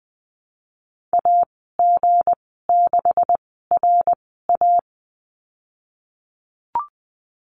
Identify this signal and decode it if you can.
Morse code, heard as a keyed tone, about 19 wpm, spaced as for 14 wpm, 700 Hz. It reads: AG6RA
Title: 167 Lesson 27g - Number 7 - Call Signs 20wpm